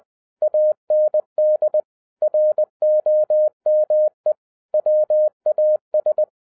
Morse code, heard as a keyed tone, about 20 wpm, 610 Hz. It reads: AND ROME WAS
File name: 2011-10-mp3 / n13-TIPS FROM MY 34 YEARS OF EXPERIENCE 20wpm